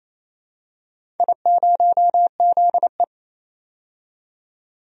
Morse code, heard as a keyed tone, about 28 wpm, 700 Hz. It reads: I0ZE
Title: DXCC-100 Call Signs - 4 Characters - 28wpm